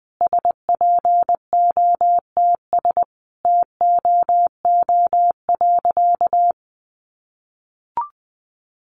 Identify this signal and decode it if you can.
Morse code, heard as a keyed tone, about 20 wpm, 700 Hz. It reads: SPOTS TOO.
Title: Additional Easy Phrases 20wpm